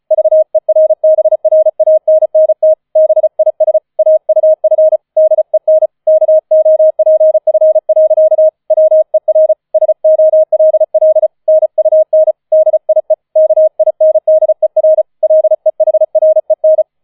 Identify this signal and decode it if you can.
Morse code, heard as a keyed tone, about 34 wpm, 630 Hz. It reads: VERBRANNT BIS AUF DEN KOPF. WER SOLL NUN DIE KINDER LEHREN
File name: 004 Vierter Streich 35wpm